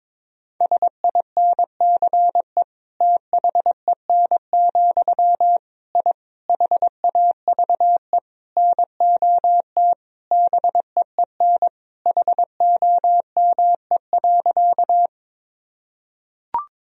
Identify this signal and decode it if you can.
Morse code, heard as a keyed tone, about 22 wpm, 700 Hz. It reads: SINCE THEN, I HAVE NOT BEEN HOME.
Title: Sentences from Top 1000 Words - Review Part 1 - 22wpm